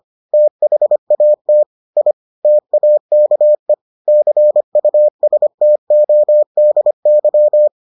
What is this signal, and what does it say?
Morse code, heard as a keyed tone, about 25 wpm, 600 Hz. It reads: THAT I TAKE CUSTODY